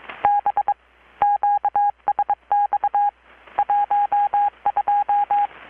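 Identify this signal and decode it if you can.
Morse code, heard as a keyed tone, about 22 wpm, 820 Hz, with radio noise. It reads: B QSX 12
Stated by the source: Morse from KPH's Night of Nights XX maritime radiotelegraphy event, received in Chiapas, Mexico about 3,800 km away on a Kenwood R-600 with a grounded 10 m long-wire antenna.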